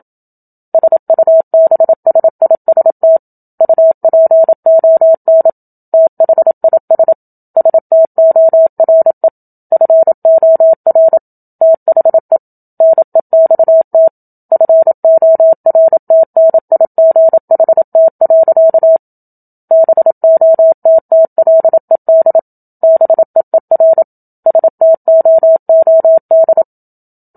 Morse code, heard as a keyed tone, about 27 wpm, 650 Hz. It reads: SUBSIST UPON THIS STORE FOR THE NEXT FORTNIGHT. BOTTLED BEER STOOD
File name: Book Two / 01-Chapter 1 - 27 WPM